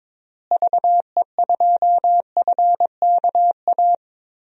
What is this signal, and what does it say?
Morse code, heard as a keyed tone, about 22 wpm, 700 Hz. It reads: VE2FKA
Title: DXCC-100 Call Signs - 6 Characters - Repeated 3x - 22wpm